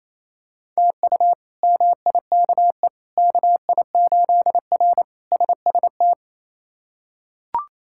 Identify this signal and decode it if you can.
Morse code, heard as a keyed tone, about 28 wpm, 700 Hz. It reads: TU MIKE KI8R SST
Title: K1USN SST QSO Practice - 28wpm